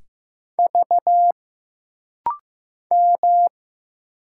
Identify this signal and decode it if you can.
Morse code, heard as a keyed tone, about 15 wpm, 700 Hz. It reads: V M
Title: Single Letters - Rapid-Fire 15wpm